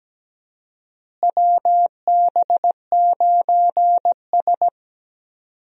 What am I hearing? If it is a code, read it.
Morse code, heard as a keyed tone, about 17 wpm, 700 Hz. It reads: WB9S
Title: USA Call Signs - 4 Characters - Repeated 2x - 17wpm